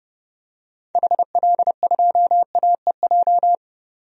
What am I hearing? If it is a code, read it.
Morse code, heard as a keyed tone, about 30 wpm, 700 Hz. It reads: HL2AEJ
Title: DXCC-100 Call Signs - 6 Characters - 30wpm